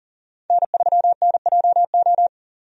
Morse code, heard as a keyed tone, about 40 wpm, 700 Hz. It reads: N3NJO
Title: USA Call Signs - 5 Characters - 40wpm